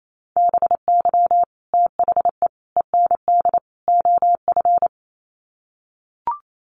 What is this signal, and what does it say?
Morse code, heard as a keyed tone, about 28 wpm, 700 Hz. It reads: BY THE END OF
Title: Sets of 4 Words 28wpm